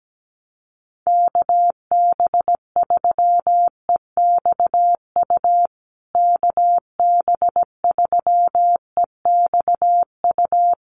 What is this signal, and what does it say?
Morse code, heard as a keyed tone, about 17 wpm, 700 Hz. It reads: KB3EXU KB3EXU
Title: USA Call Signs Repeated 2x 17wpm